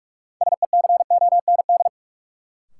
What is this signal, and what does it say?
Morse code, heard as a keyed tone, about 42 wpm, 700 Hz. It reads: SECOND
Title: Sentences from Top 300 Words - New Word Drill - 45wpm